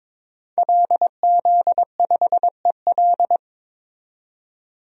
Morse code, heard as a keyed tone, about 22 wpm, 700 Hz. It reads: LZ5EL
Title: DXCC-100 Call Signs - 5 Characters - 22wpm